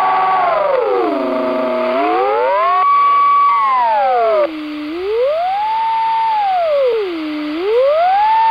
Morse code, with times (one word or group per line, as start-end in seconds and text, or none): none